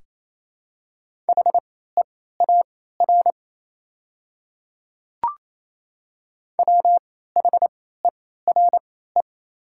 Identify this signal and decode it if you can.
Morse code, heard as a keyed tone, about 27 wpm, spaced as for 16 wpm, 700 Hz. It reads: HEAR WHERE
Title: 3x Character Spacing - Top 300 Words - 28wpm